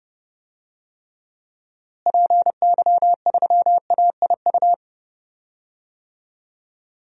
Morse code, heard as a keyed tone, about 32 wpm, 700 Hz. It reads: PY3AIU
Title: DXCC-100 Call Signs - 6 Characters - Repeated 3x - 30wpm